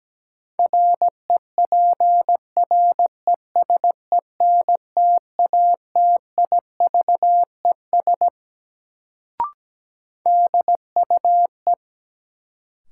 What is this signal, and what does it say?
Morse code, heard as a keyed tone, about 17 wpm, 700 Hz. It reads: REPRESENTATIVES DUE
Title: Sets of 5 Words Encore - New Word Drill - 17wpm